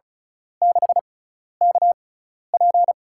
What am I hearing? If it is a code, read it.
Morse code, heard as a keyed tone, about 36 wpm, spaced as for 12 wpm, 700 Hz. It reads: 6KP
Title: USA Call Signs - 6x Character Spacing - 35wpm